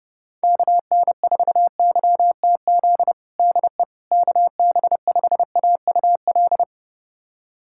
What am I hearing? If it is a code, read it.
Morse code, heard as a keyed tone, about 30 wpm, 700 Hz. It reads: KN4YTZ DE KB5AUL
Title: USA Callsign DE Callsign - 6 Characters - 30wpm